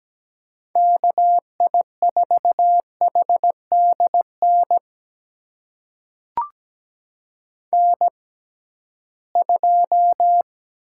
Morse code, heard as a keyed tone, about 17 wpm, spaced as for 13 wpm, 700 Hz. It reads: KI4HDN N2
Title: USA Call Signs - 6x Character Spacing - 17wpm